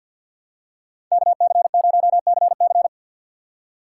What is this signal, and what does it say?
Morse code, heard as a keyed tone, about 53 wpm, 700 Hz. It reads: KC0CC
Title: USA Call Signs - 3x Character Spacing - 50wpm